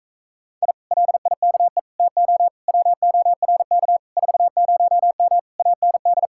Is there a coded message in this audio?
Yes: Morse code from 0.6 to 6.3 s, heard as a keyed tone, about 46 wpm, 700 Hz. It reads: I LIKE TO WORK 40M AND